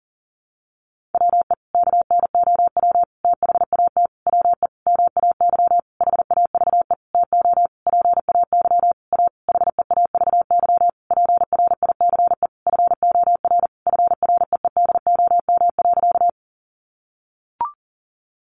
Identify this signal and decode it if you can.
Morse code, heard as a keyed tone, about 40 wpm, 700 Hz. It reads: WE KNOW THAT WE MAY HAVE TO PAY A HEAVY PRICE FOR FREEDOM.
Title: Sentences from Top 1100 Words 40wpm